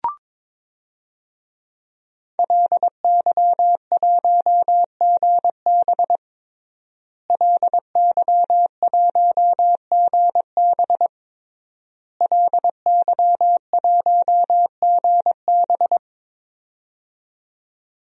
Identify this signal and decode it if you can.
Morse code, heard as a keyed tone, about 21 wpm, 700 Hz. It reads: LY1GB LY1GB LY1GB
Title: DXCC-100 Call Signs - 5 Characters - Repeated 3x - 22wpm